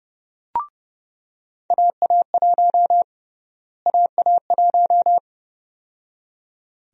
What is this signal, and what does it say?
Morse code, heard as a keyed tone, about 30 wpm, 700 Hz. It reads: AA1 AA1
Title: USA Call Sign Prefixes - Repeated 2x - 30wpm